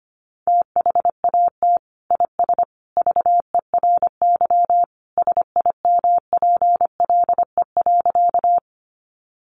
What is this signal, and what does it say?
Morse code, heard as a keyed tone, about 25 wpm, 700 Hz. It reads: THAT IS VERY SIMPLE.
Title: Sentences from Top 400 Words 25wpm